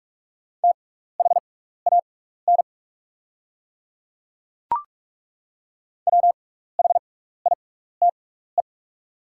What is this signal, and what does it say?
Morse code, heard as a keyed tone, about 42 wpm, spaced as for 15 wpm, 690 Hz. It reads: THAN WHITE
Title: 6x Character Spacing - Top 300 Words - 45wpm